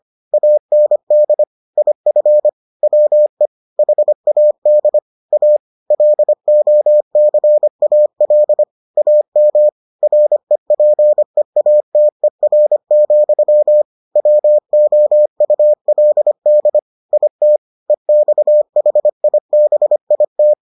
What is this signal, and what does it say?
Morse code, heard as a keyed tone, about 25 wpm, 600 Hz. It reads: AND IF WE HAD A LOCAL AM REPEATER, WOULD IT EXHIBIT